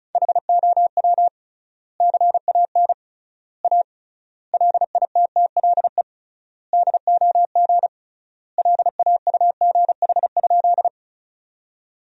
Morse code, heard as a keyed tone, about 36 wpm, 700 Hz. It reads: HOW CAN A LITTLE DOG LAUGH?
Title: 3x Word-Spacing - Sentences from Top 500 Words - 35wpm